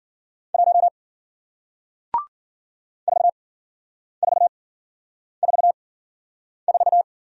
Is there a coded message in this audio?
Yes: Morse code from 0.5 to 7.0 s, heard as a keyed tone, about 53 wpm, spaced as for 9 wpm, 700 Hz. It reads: 1 4444